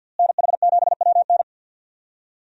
Morse code, heard as a keyed tone, about 49 wpm, 700 Hz. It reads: NH7WN
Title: USA Call Signs - 5 Characters - Repeated 2x - 50wpm